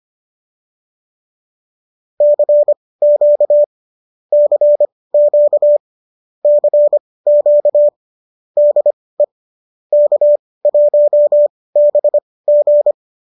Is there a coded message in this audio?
Yes: Morse code from 2.2 to 12.9 s, heard as a keyed tone, about 25 wpm, spaced as for 18 wpm, 590 Hz. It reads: CQ CQ CQ DE K1BG